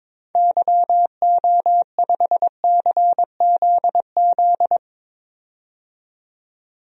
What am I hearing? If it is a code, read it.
Morse code, heard as a keyed tone, about 21 wpm, 700 Hz. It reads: YO5CZZ
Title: DXCC-100 Call Signs - 6 Characters - 22wpm